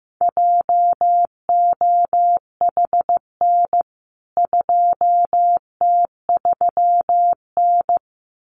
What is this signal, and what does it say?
Morse code, heard as a keyed tone, about 15 wpm, 700 Hz. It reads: JOHN 2T3N
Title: CWT Exchange 15wpm